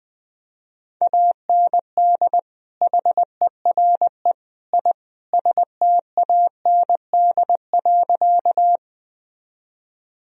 Morse code, heard as a keyed tone, about 20 wpm, 700 Hz. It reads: AND HERE I STAND.
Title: Sentences from Top 1000 Words - Review Part 1 - 20wpm